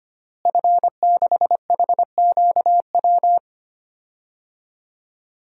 Morse code, heard as a keyed tone, about 25 wpm, 700 Hz. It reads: F6HQW
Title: DXCC-100 Call Signs - 5 Characters - Repeated 2x - 25wpm